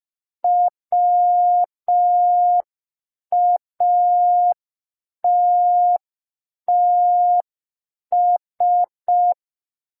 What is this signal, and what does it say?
Morse code, heard as a keyed tone, about 5 wpm, 710 Hz. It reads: WATTS
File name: g3zpd / Morse code P4